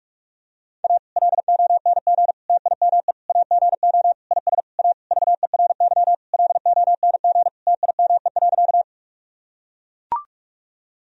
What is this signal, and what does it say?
Morse code, heard as a keyed tone, about 41 wpm, 700 Hz. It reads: A LONG TIME AGO IS A VERY LONG TIME.
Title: Sentences from Top 1000 Words - Review Part 3 - 45wpm